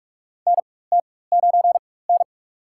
Morse code, heard as a keyed tone, about 46 wpm, spaced as for 21 wpm, 700 Hz. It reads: NT9N
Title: USA Call Signs - 4 Characters - 4x Character Spacing - 45wpm